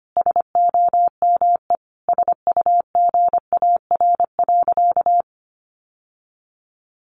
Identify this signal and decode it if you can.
Morse code, heard as a keyed tone, about 25 wpm, 700 Hz. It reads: SOME SUGAR.
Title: Sentences from Top 1200 Words 25wpm